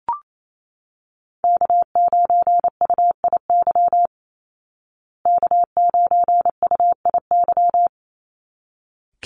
Morse code, heard as a keyed tone, about 28 wpm, 700 Hz. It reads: K9UIY K9UIY
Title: Call Signs Repeated 3x - 28wpm